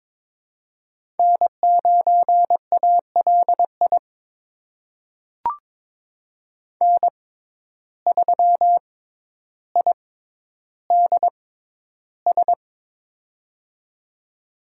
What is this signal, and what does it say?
Morse code, heard as a keyed tone, about 22 wpm, spaced as for 11 wpm, 700 Hz. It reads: N9ALI N3IDS
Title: USA Call Signs - 5 Characters - 6x Character Spacing - 22wpm